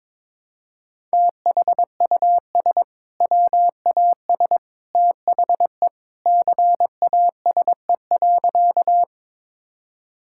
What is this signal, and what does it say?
Morse code, heard as a keyed tone, about 22 wpm, 700 Hz. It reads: THUS WAS THE CASE.
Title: Sentences from Top 1000 Words - Review Part 5 - 22wpm